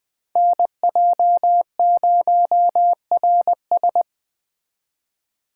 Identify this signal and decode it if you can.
Morse code, heard as a keyed tone, about 20 wpm, 700 Hz. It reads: NJ0RS
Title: Speed Racing - USA 2x2 Call Signs 17wpm